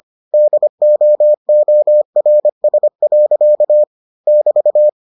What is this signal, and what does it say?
Morse code, heard as a keyed tone, about 25 wpm, 600 Hz. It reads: DOORS. <BT>